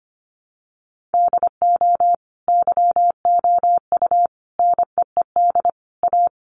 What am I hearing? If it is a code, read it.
Morse code, heard as a keyed tone, about 25 wpm, 700 Hz. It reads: DO YOU NEED A
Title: Sentences from Top 1500 Words 25wpm